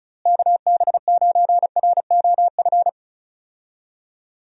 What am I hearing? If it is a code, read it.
Morse code, heard as a keyed tone, about 34 wpm, 700 Hz. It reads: KB9ROF
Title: USA Call Signs - 6 Characters - Repeated 3x - 35wpm